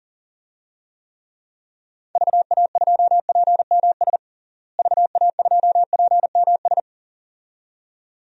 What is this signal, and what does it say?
Morse code, heard as a keyed tone, about 37 wpm, 700 Hz. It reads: VA2PMS VA2PMS
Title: DXCC-100 Call Signs - 6 Characters - Repeated 2x - 40wpm